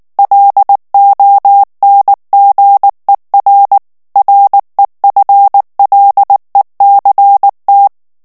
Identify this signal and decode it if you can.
Morse code, heard as a keyed tone, about 19 wpm, 800 Hz. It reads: LONGER REFLECT
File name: ANGLAIS texte clair 3 vitesse 18 mots minute